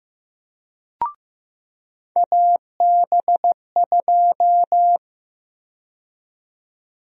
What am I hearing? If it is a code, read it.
Morse code, heard as a keyed tone, about 15 wpm, 700 Hz. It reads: AB2
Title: USA Call Sign Prefixes - 15wpm